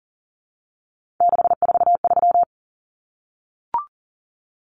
Morse code, heard as a keyed tone, about 40 wpm, 700 Hz. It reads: 643
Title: Common Cut Numbers - 3-Digit 40wpm